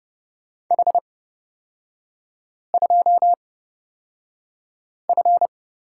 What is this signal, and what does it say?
Morse code, heard as a keyed tone, about 30 wpm, spaced as for 5 wpm, 700 Hz. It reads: H2F
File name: Lesson 36e - BT - Three Characters 30wpm